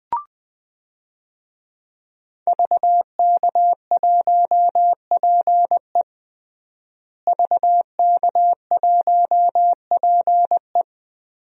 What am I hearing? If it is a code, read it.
Morse code, heard as a keyed tone, about 20 wpm, 700 Hz. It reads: VK1PE VK1PE
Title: DXCC-100 Call Signs - 5 Characters - Repeated 3x - 20wpm